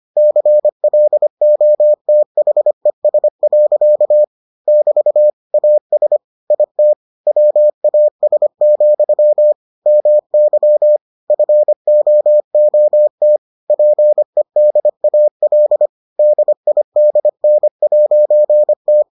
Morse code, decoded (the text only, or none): CLOTHES. <BT> AS IT WAS, MY FOOT PEDAL DIDN'T